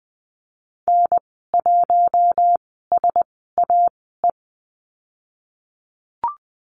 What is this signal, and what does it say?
Morse code, heard as a keyed tone, about 20 wpm, spaced as for 14 wpm, 700 Hz. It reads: N1SAE
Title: 069 Lesson 12g - Letter d - Call Signs 20wpm